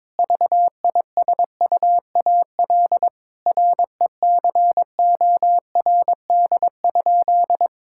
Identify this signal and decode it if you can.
Morse code, heard as a keyed tone, about 21 wpm, 700 Hz. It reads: VISUAL RECORD?
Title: Sentences from Top 1000 Words - Review Part 6 - 22wpm